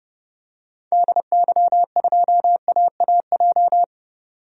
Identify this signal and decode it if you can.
Morse code, heard as a keyed tone, about 30 wpm, 700 Hz. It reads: DY2AAJ
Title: DXCC-100 Call Signs - 6 Characters - 30wpm